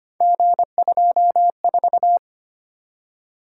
Morse code, heard as a keyed tone, about 25 wpm, 700 Hz. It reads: G24
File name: Three Letter-Number Combinations - 25wpm